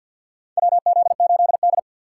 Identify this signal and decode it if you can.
Morse code, heard as a keyed tone, about 53 wpm, 700 Hz. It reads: WZ8D
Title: USA Call Signs - 4 Characters - 3x Character Spacing - 50wpm